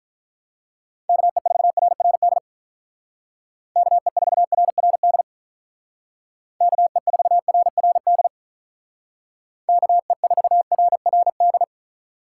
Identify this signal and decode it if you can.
Morse code, heard as a keyed tone, about 44 wpm, 700 Hz. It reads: KE4RRD KE4RRD KE4RRD KE4RRD
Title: Speed Racing - USA 2x3 Call Signs 35wpm